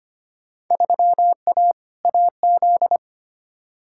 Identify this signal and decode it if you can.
Morse code, heard as a keyed tone, about 25 wpm, 700 Hz. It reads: 3A AZ
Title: Field Day Exchanges - 25wpm